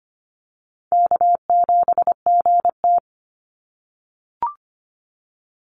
Morse code, heard as a keyed tone, about 25 wpm, 700 Hz. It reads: K7GT
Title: NA SOTA Chaser Callsigns 25wpm